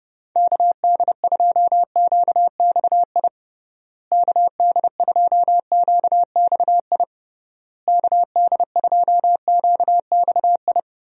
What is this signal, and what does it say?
Morse code, heard as a keyed tone, about 30 wpm, 700 Hz. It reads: KD2QXI KD2QXI KD2QXI
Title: USA Call Signs - 6 Characters - Repeated 3x - 30wpm